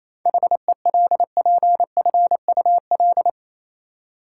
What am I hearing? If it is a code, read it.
Morse code, heard as a keyed tone, about 28 wpm, 700 Hz. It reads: HELPFUL
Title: Seven Letter Words - 28wpm